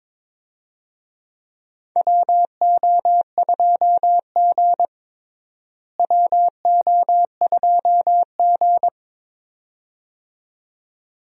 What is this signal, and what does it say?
Morse code, heard as a keyed tone, about 21 wpm, 700 Hz. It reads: WO2G WO2G
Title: USA Call Signs - 4 Characters - Repeated 2x - 22wpm